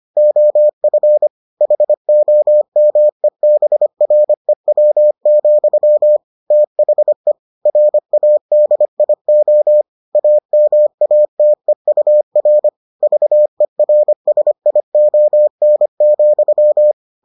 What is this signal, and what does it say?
Morse code, heard as a keyed tone, about 25 wpm, 600 Hz. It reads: OF HOMEBREW, THE RADIO AMATEUR VERSION,